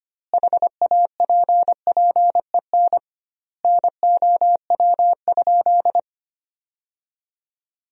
Morse code, heard as a keyed tone, about 25 wpm, 700 Hz. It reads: HAPPEN NOW?
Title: 2x Word-Spacing - Sentences from Top 400 Words - 25wpm